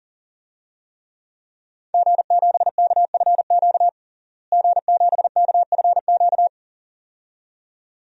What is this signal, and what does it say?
Morse code, heard as a keyed tone, about 40 wpm, 700 Hz. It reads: G7KFQ G7KFQ
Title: DXCC-100 Call Signs - 5 Characters - Repeated 2x - 40wpm